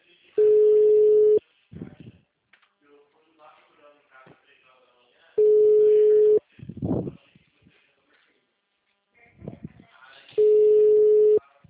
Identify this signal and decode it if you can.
Ringback tone (Europe, 425 Hz).
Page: 66.94.108.90 TLML PRUDENTE 1